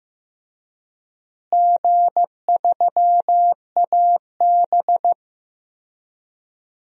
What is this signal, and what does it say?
Morse code, heard as a keyed tone, about 15 wpm, 700 Hz. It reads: G3AB
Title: DXCC-100 Call Signs - 4 Characters - Repeated 3x - 15wpm